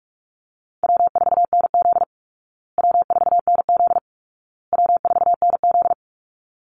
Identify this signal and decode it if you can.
Morse code, heard as a keyed tone, about 44 wpm, 700 Hz. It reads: W4NZ W4NZ W4NZ
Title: Call Signs Repeated 2x - 45wpm